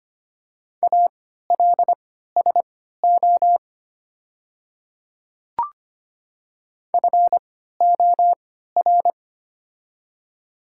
Morse code, heard as a keyed tone, about 25 wpm, spaced as for 14 wpm, 700 Hz. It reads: ALSO FOR
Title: 3x Character Spacing - Top 300 Words - 25wpm